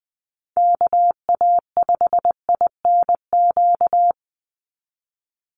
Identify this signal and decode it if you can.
Morse code, heard as a keyed tone, about 20 wpm, 700 Hz. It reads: KA5INQ
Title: USA Call Signs 20wpm